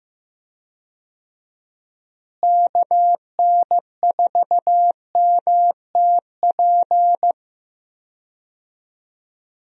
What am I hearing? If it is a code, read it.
Morse code, heard as a keyed tone, about 15 wpm, 700 Hz. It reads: KN4MTP
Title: USA Call Signs - 6 Characters - Repeated 3x - 15wpm